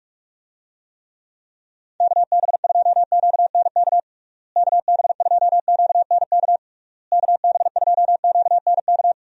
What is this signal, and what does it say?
Morse code, heard as a keyed tone, about 42 wpm, 700 Hz. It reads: KB2QNK KB2QNK KB2QNK
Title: USA Call Signs - 6 Characters - Repeated 3x - 45wpm